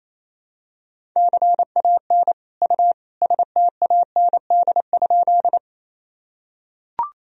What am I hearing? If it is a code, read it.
Morse code, heard as a keyed tone, about 28 wpm, 700 Hz. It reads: CAN U STAND?
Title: Sentences from Top 300 Words and Abbreviations 28wpm